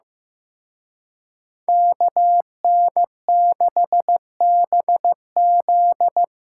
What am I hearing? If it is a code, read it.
Morse code, heard as a keyed tone, about 18 wpm, 700 Hz. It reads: KN6BZ
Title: Speed Racing - USA Call Signs 15wpm